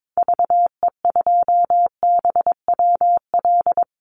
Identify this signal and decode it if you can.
Morse code, heard as a keyed tone, about 22 wpm, 700 Hz. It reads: VE2BWL
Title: Call Signs - 22wpm